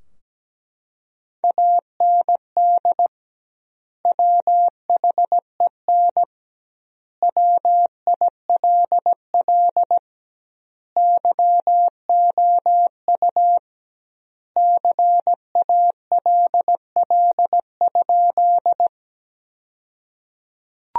Morse code, heard as a keyed tone, about 17 wpm, 700 Hz. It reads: AND WHEN WILL YOU CALL?
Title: 2x Word-Spacing - Sentences from Top 100 Words - 17wpm